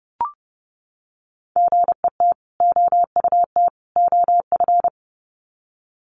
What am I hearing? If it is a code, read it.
Morse code, heard as a keyed tone, about 30 wpm, 700 Hz. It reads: GET OUT OF
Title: Sets of 3 Words 30wpm